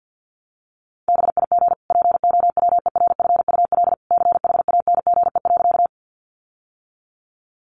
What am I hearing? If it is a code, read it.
Morse code, heard as a keyed tone, about 50 wpm, 700 Hz. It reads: BIG POWERFUL CHANGE.
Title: Sentences from Top 1300 Words 50wpm